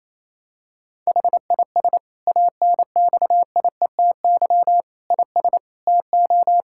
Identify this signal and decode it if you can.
Morse code, heard as a keyed tone, about 28 wpm, 700 Hz. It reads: HIS ANXIETY IS TO
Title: Sentences from Top 1600 Words 28wpm